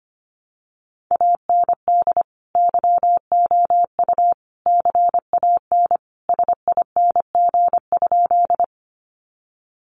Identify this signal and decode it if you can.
Morse code, heard as a keyed tone, about 25 wpm, 700 Hz. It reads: AND YOU CAN SING?
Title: Sentences from Top 400 Words 25wpm